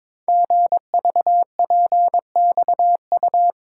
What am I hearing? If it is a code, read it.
Morse code, heard as a keyed tone, about 22 wpm, 700 Hz. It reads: GVPXU
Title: Code Groups - 5 Letters - 22wpm